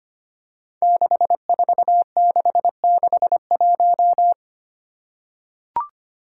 Morse code, heard as a keyed tone, about 25 wpm, 700 Hz. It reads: 64661
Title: Five Challenging Letter-Number Combinations 25wpm